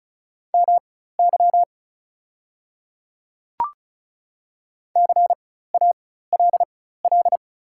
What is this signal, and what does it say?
Morse code, heard as a keyed tone, about 34 wpm, spaced as for 16 wpm, 690 Hz. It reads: MY CALL
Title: 4x Character Spacing - Top 100 Words - 35wpm